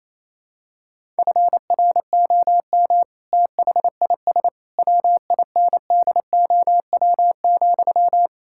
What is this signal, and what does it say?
Morse code, heard as a keyed tone, about 28 wpm, 700 Hz. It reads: FROM THIS WINDOW,
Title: Sentences from Top 1600 Words 28wpm